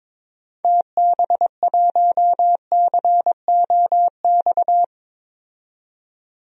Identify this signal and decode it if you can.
Morse code, heard as a keyed tone, about 22 wpm, 700 Hz. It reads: TB1COX
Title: DXCC-100 Call Signs - 6 Characters - 22wpm